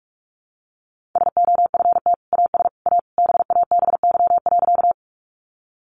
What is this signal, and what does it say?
Morse code, heard as a keyed tone, about 44 wpm, 700 Hz. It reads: SOFT AS A BABY.
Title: Sentences from Top 1500 Words 45wpm